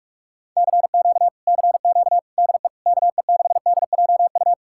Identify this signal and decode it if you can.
Morse code, heard as a keyed tone, about 46 wpm, 700 Hz. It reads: CQ CQ DE KE6DJU